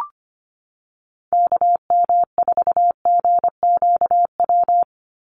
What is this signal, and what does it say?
Morse code, heard as a keyed tone, about 25 wpm, 700 Hz. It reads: KM4GQW
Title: USA Call Signs 25wpm